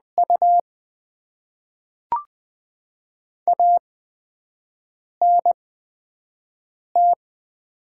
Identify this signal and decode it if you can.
Morse code, heard as a keyed tone, about 20 wpm, spaced as for 5 wpm, 700 Hz. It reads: U ANT
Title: Top 20 QSO Elements - 8x Character Spacing - 20wpm